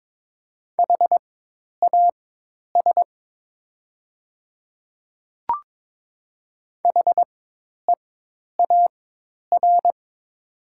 Morse code, heard as a keyed tone, about 22 wpm, spaced as for 10 wpm, 700 Hz. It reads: HAS HEAR
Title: 4x Character Spacing - Top 300 Words - 22wpm